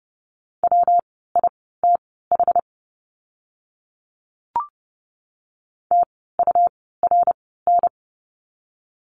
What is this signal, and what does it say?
Morse code, heard as a keyed tone, about 30 wpm, spaced as for 17 wpm, 700 Hz. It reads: WITH TURN
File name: course / 30:10wpm / 230 Lesson 37f - Letter x - Words 30wpm